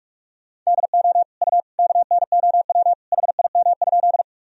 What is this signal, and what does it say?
Morse code, heard as a keyed tone, about 46 wpm, 700 Hz. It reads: DO U KNOW HIM?
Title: Sentences from Top 100 Words and Abbreviations 45wpm